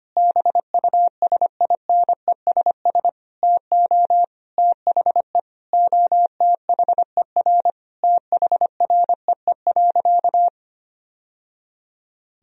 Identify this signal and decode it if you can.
Morse code, heard as a keyed tone, about 25 wpm, 700 Hz. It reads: BUSINESS TO THE OTHER THREE.
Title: Sentences from Top 1000 Words - Review Part 4 - 25wpm